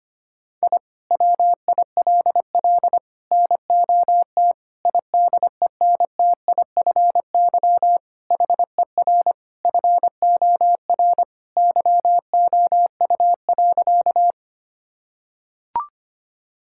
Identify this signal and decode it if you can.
Morse code, heard as a keyed tone, about 25 wpm, 700 Hz. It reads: I WILL NOT IDENTIFY HER FOR YOU.
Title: Sentences from Top 1000 Words - Review Part 6 - 25wpm